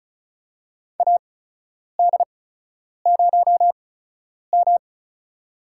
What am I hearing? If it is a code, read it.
Morse code, heard as a keyed tone, about 34 wpm, spaced as for 10 wpm, 700 Hz. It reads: AD0M
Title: USA Call Signs - 4 Characters - 8x Character Spacing - 35wpm